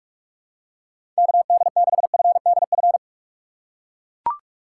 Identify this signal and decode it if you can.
Morse code, heard as a keyed tone, about 42 wpm, 700 Hz. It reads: KD6FDF
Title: USA Call Signs - 6 Characters - 6x Character Spacing - 45wpm